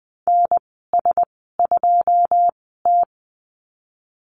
Morse code, heard as a keyed tone, about 20 wpm, spaced as for 15 wpm, 700 Hz. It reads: NS2T
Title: 154 Lesson 25g - Letter p - Call Signs 20wpm